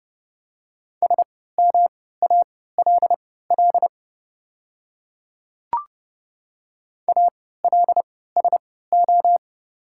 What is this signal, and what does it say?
Morse code, heard as a keyed tone, about 30 wpm, spaced as for 17 wpm, 700 Hz. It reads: SMALL ALSO
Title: 3x Character Spacing - Top 300 Words - 30wpm